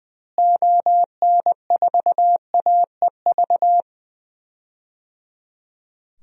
Morse code, heard as a keyed tone, about 20 wpm, 700 Hz. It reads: ON4AEV
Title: DXCC-100 Call Signs - 6 Characters - Repeated 3x - 20wpm